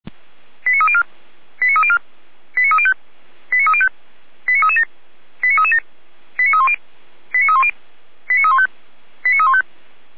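Five-tone selective call (ZVEI), heard as quick runs of five tones, tones of about 70 ms. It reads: 89304 89304 89305 89305 89337 89337 89310 89310 89315 89315
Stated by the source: ZVEI five-tone call received by SDR.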